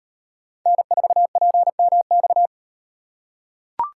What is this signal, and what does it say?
Morse code, heard as a keyed tone, about 36 wpm, 700 Hz. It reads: N4PMX